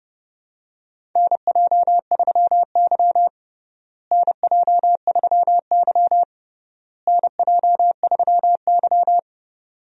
Morse code, heard as a keyed tone, about 32 wpm, 700 Hz. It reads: NJ3Y NJ3Y NJ3Y
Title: USA Call Signs - 4 Characters - Repeated 3x - 30wpm